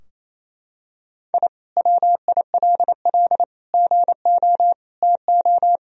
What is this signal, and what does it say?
Morse code, heard as a keyed tone, about 27 wpm, 700 Hz. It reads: I WILL GO TO
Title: Sentences from Top 1000 Words - Review Part 2 - 28wpm